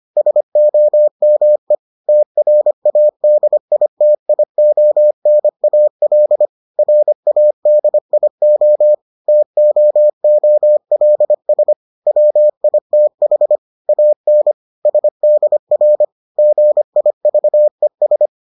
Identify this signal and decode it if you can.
Morse code, heard as a keyed tone, about 25 wpm, 600 Hz. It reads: SOME TRADITIONAL RADIO TOOLS WITH AN SDR GIVES